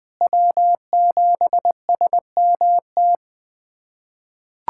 Morse code, heard as a keyed tone, about 20 wpm, 700 Hz. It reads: W7SMT
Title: USA Callsign DE Callsign - 5 Characters - 20wpm